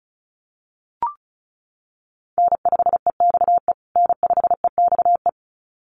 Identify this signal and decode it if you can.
Morse code, heard as a keyed tone, about 34 wpm, 700 Hz. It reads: N5EXE N5EXE
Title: USA Call Signs Repeated 2x 35wpm